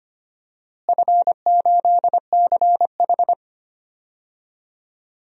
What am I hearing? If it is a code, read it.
Morse code, heard as a keyed tone, about 25 wpm, 700 Hz. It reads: F8CH
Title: DXCC-100 Call Signs - 4 Characters - Repeated 2x - 25wpm